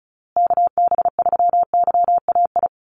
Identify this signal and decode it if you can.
Morse code, heard as a keyed tone, about 34 wpm, 700 Hz. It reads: KB3YAI
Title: USA Call Signs Repeated 2x 35wpm